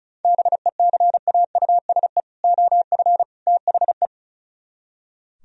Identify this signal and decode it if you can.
Morse code, heard as a keyed tone, about 44 wpm, 700 Hz. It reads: BECAUSE OF THE